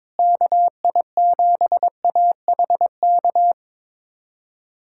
Morse code, heard as a keyed tone, about 22 wpm, 700 Hz. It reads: KI7AHK
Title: USA Callsign DE Callsign 22wpm